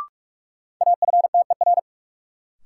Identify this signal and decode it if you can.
Morse code, heard as a keyed tone, about 45 wpm, 700 Hz. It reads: AFTER
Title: Sentences from Top 200 Words - New Word Drill - 45wpm